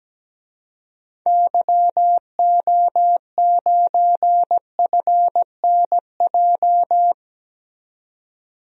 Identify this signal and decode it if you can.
Morse code, heard as a keyed tone, about 17 wpm, 700 Hz. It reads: YO9FNJ
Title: DXCC-100 Call Signs - 6 Characters - 17wpm